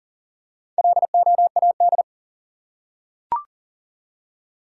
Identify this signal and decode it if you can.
Morse code, heard as a keyed tone, about 38 wpm, 700 Hz. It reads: LOAD
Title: Four Letter Words - 40wpm